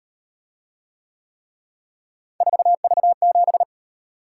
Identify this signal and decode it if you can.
Morse code, heard as a keyed tone, about 37 wpm, 700 Hz. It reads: 4V7